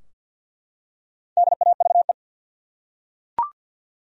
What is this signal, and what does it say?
Morse code, heard as a keyed tone, about 47 wpm, 700 Hz. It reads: DAVE
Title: USA Common Names - Max 5 Letters - 50wpm